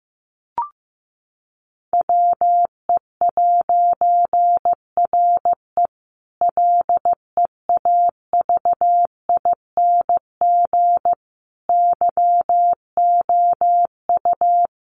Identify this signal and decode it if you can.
Morse code, heard as a keyed tone, about 15 wpm, 700 Hz. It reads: WE'RE LEAVING YOU
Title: A Sound Of Thunder 15wpm